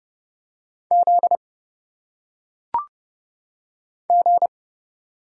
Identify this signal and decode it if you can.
Morse code, heard as a keyed tone, about 30 wpm, 700 Hz. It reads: Z G
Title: Single Letters - Phonetic Pronunciation 30wpm